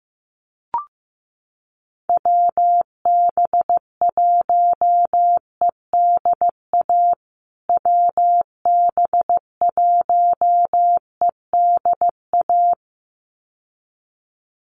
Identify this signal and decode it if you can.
Morse code, heard as a keyed tone, about 15 wpm, 700 Hz. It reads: WB1EDA WB1EDA
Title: USA Call Signs Repeated 2x 15wpm